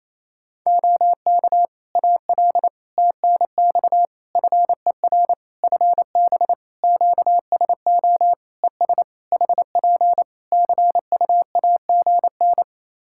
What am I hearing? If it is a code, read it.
Morse code, heard as a keyed tone, about 28 wpm, 700 Hz. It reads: OK AL TNX FER FB QSO ES HP CUAGN
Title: Practice QSO - Exchange 4 - Ending - 28wpm